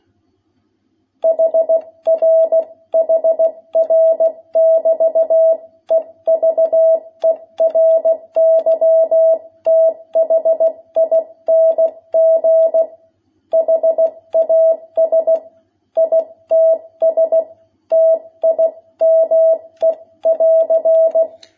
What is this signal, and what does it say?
Morse code, heard as a keyed tone, about 16 wpm, 660 Hz. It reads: HR HR <BT> EVERYTHING HAS ITS TIME <AR>
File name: jh8cbh / 2021cw